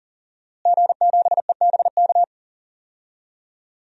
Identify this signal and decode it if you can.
Morse code, heard as a keyed tone, about 40 wpm, 700 Hz. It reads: G7EBK
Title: DXCC-100 Call Signs - 5 Characters - Repeated 2x - 40wpm